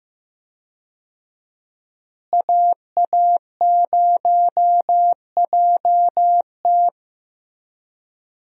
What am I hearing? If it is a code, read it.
Morse code, heard as a keyed tone, about 15 wpm, 700 Hz. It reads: AA0JT
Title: USA Call Signs - 5 Characters - Repeated 2x - 15wpm